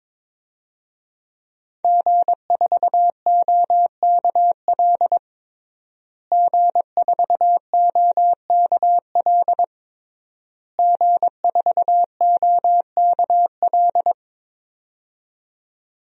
Morse code, heard as a keyed tone, about 22 wpm, 700 Hz. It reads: G4OKL G4OKL G4OKL
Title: DXCC-100 Call Signs - 5 Characters - Repeated 3x - 22wpm